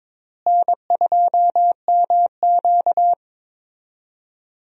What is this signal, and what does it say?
Morse code, heard as a keyed tone, about 22 wpm, 700 Hz. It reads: N2MQ
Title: USA Call Signs - 4 Characters - Repeated 3x - 22wpm